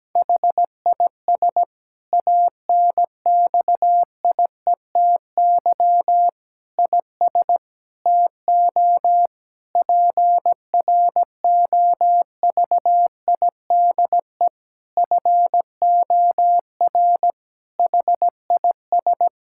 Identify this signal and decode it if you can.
Morse code, heard as a keyed tone, about 17 wpm, 700 Hz. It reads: HIS ANXIETY IS TO PROVIDE FOR HIS
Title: Sentences from Top 1600 Words 17wpm